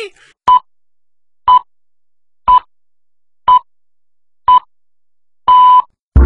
Radio time signal (time pips). It is heard from 0.5 to 5.8 s.